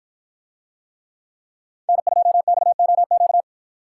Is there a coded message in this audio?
Yes: Morse code from 1.9 to 3.4 s, heard as a keyed tone, about 53 wpm, 700 Hz. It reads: N2XOQ